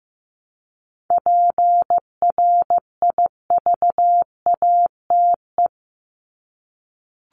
Morse code, heard as a keyed tone, about 15 wpm, 700 Hz. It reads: PRIVATE
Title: Top 751-1000 Words 15wpm